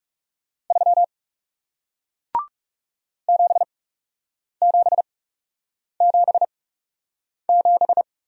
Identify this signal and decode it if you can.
Morse code, heard as a keyed tone, about 39 wpm, spaced as for 8 wpm, 700 Hz. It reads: 3 7777